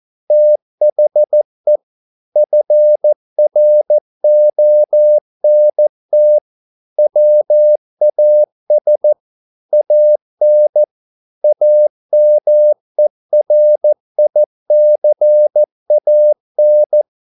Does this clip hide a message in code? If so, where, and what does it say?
Yes: Morse code from 0.3 to 17.0 s, heard as a keyed tone, about 14 wpm, 600 Hz. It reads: THE FRONT WAS AN AMERICAN